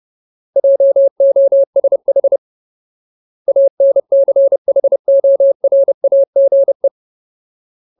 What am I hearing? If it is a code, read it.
Morse code, heard as a keyed tone, about 30 wpm, 550 Hz. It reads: JOSH ANCHORAGE
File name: Ultimate Word List / 30WPM Ultimate